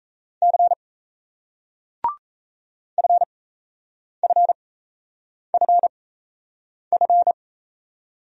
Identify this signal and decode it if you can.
Morse code, heard as a keyed tone, about 38 wpm, spaced as for 8 wpm, 700 Hz. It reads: C FFFF